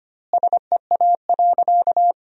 Morse code, heard as a keyed tone, about 25 wpm, 700 Hz. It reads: SEA.
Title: 2x Word-Spacing - Sentences from Top 300 Words - 25wpm